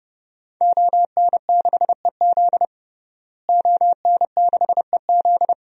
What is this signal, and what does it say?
Morse code, heard as a keyed tone, about 30 wpm, 700 Hz. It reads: ON6EZ ON6EZ
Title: DXCC-100 Call Signs - 5 Characters - Repeated 3x - 30wpm